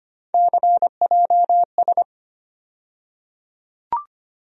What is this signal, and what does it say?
Morse code, heard as a keyed tone, about 25 wpm, 700 Hz. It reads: CJS
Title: USA Call Sign Postfixes - 3 Letters 25wpm